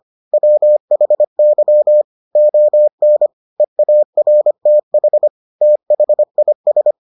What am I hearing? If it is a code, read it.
Morse code, heard as a keyed tone, about 25 wpm, 600 Hz. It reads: WHY ON EARTH THIS